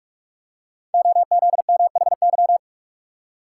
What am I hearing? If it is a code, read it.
Morse code, heard as a keyed tone, about 44 wpm, 700 Hz. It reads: OZMHY
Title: Code Groups - 5 Letters - 45wpm